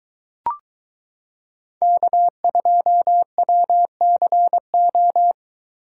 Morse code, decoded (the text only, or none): K2WCO